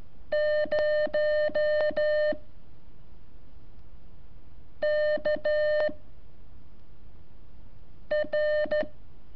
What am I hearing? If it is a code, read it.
Morse code, heard as a keyed tone, about 12 wpm, 600 Hz. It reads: 0 K R
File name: curso / ejercicio 22